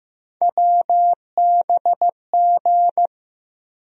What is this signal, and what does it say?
Morse code, heard as a keyed tone, about 15 wpm, 700 Hz. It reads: WBG
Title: USA Call Sign Postfixes - 3 Letters 15wpm